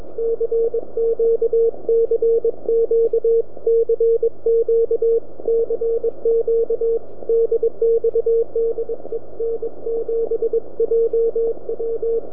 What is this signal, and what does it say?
Morse code, heard as a keyed tone, about 21 wpm, 460 Hz, with radio noise. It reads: CQ CQ CQ CQ DX DE N7JW
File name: Recordings / n7jw 160 0810 7jan05